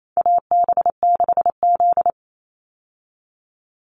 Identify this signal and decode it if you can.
Morse code, heard as a keyed tone, about 28 wpm, 700 Hz. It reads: AB6Z
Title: Call Signs Repeated 3x - 28wpm